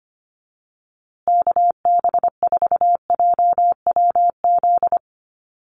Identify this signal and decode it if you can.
Morse code, heard as a keyed tone, about 25 wpm, 700 Hz. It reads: KB4JWZ
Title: USA Call Signs Repeated 2x 25wpm